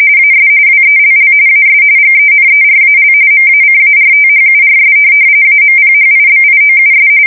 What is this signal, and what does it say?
RTTY, about 45.45 baud. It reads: AS WELL AS GENERAL MACARTHURS AMPHIBIOUS
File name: fsk170 / USS-New-Jersey